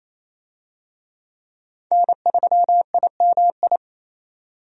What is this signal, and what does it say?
Morse code, heard as a keyed tone, about 28 wpm, 700 Hz. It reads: N3IMI
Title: USA Call Signs - 5 Characters - 28wpm